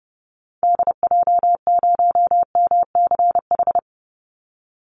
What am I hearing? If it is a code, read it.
Morse code, heard as a keyed tone, about 30 wpm, 700 Hz. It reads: DJ0MCH
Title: Call Signs Repeated 3x - 30wpm